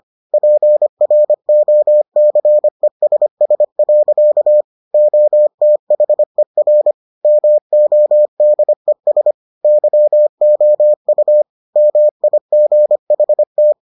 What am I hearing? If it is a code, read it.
Morse code, heard as a keyed tone, about 25 wpm, 600 Hz. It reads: PROCESS. OTHER MODES YOU MIGHT